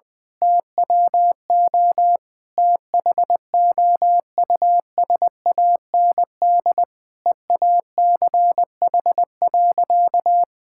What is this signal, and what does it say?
Morse code, heard as a keyed tone, about 20 wpm, 700 Hz. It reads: TWO THOUSAND EACH.
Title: Sentences from Top 1000 Words - Review Part 5 - 20wpm